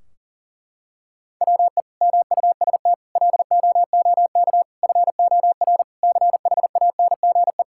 Morse code, heard as a keyed tone, about 37 wpm, 700 Hz. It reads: WE MUST LOOK FOR CHANGE
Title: Sentences from Top 1600 Words 40wpm